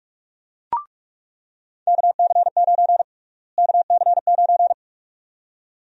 Morse code, heard as a keyed tone, about 47 wpm, 700 Hz. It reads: KC9 KC9
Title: USA Call Sign Prefixes - Repeated 2x - 45wpm